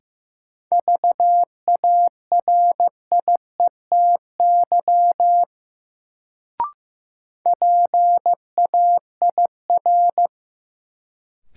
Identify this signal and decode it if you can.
Morse code, heard as a keyed tone, about 15 wpm, 700 Hz. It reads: VARIETY PAIR
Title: Sets of 3 Words - New Word Drill - 15wpm